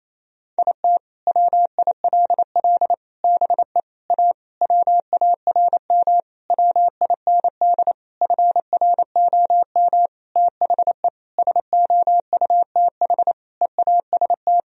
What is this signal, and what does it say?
Morse code, heard as a keyed tone, about 28 wpm, 700 Hz. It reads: IT WILL BE A WARM WIND FROM THE SOUTH EAST
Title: Sentences from Top 1000 Words - Review Part 3 - 28wpm